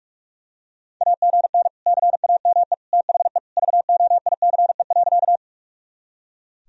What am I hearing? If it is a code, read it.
Morse code, heard as a keyed tone, about 46 wpm, 700 Hz. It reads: AGN CAME THE VOICE.
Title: Sentences from Top 500 Words and Abbreviations 45wpm